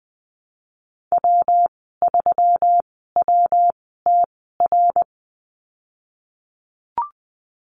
Morse code, heard as a keyed tone, about 20 wpm, spaced as for 15 wpm, 700 Hz. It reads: W3WTR
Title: 122 Lesson 20g - Number 3 - Call Signs 20wpm